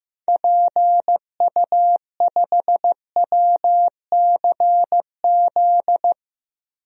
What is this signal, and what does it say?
Morse code, heard as a keyed tone, about 16 wpm, 700 Hz. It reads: PU5WCZ
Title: DXCC-100 Call Signs - 6 Characters - Repeated 3x - 15wpm